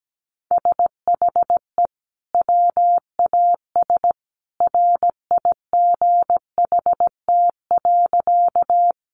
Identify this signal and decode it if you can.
Morse code, heard as a keyed tone, about 17 wpm, 700 Hz. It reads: SHE WAS RIGHT.
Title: Sentences from Top 200 Words 17wpm